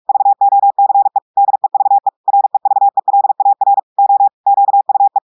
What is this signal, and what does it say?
Morse code, heard as a keyed tone, about 46 wpm, 800 Hz. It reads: VOCE DEVE REVELAR O QUE
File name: audios / portugais / portugais texte en clair 2 vitesse 45 mots minutes